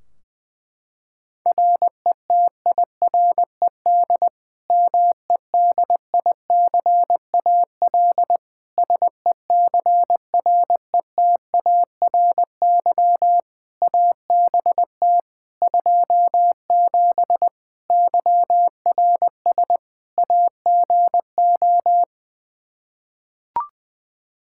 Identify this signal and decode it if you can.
Morse code, heard as a keyed tone, about 20 wpm, 700 Hz. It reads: RETIRED MEDICAL SECRETARY ABT 27 YRS AGO
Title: Practice QSO - Exchange 3 - More Info - 20wpm